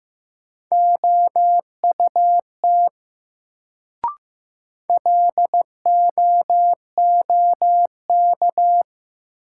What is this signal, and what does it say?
Morse code, heard as a keyed tone, about 15 wpm, 700 Hz. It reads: OUT LOOK
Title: Sentences from Top 100 Words - New Word Drill - 15wpm